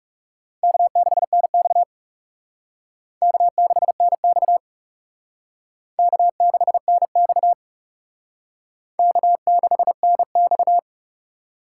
Morse code, heard as a keyed tone, about 39 wpm, 700 Hz. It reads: K6NX K6NX K6NX K6NX